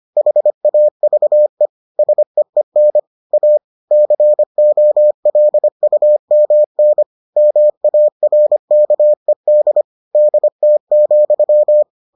Morse code, heard as a keyed tone, about 25 wpm, 600 Hz. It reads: HAVE SEEN A COLUMN MARKED DT,